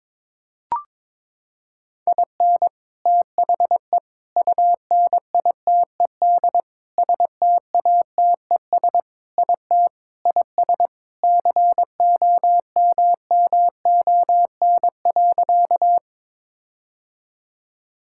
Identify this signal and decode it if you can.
Morse code, heard as a keyed tone, about 22 wpm, 700 Hz. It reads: IN THE UNITED STATES IT IS COMMON.
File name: practice / 22wpm / Sentences with 4-Word Phrases - 22wpm